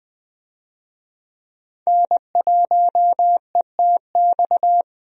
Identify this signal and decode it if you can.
Morse code, heard as a keyed tone, about 21 wpm, 700 Hz. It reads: N1ETX
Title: USA Call Signs - 5 Characters - Repeated 3x - 20wpm